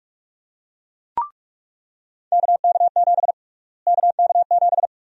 Morse code, heard as a keyed tone, about 41 wpm, 700 Hz. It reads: KK7 KK7
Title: USA Call Sign Prefixes - Repeated 2x - 45wpm